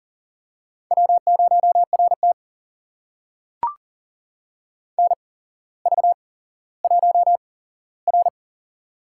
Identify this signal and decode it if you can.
Morse code, heard as a keyed tone, about 39 wpm, spaced as for 17 wpm, 690 Hz. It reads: W0RT NV1R
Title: USA Call Signs - 4 Characters - 8x Character Spacing - 40wpm